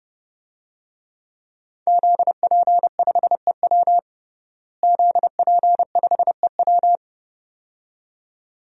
Morse code, heard as a keyed tone, about 30 wpm, 700 Hz. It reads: ZP5EW ZP5EW
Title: DXCC-100 Call Signs - 5 Characters - Repeated 2x - 30wpm